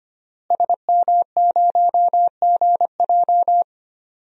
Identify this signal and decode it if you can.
Morse code, heard as a keyed tone, about 25 wpm, 700 Hz. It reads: SM0GJ
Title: DXCC-100 Call Signs - 5 Characters - Repeated 3x - 25wpm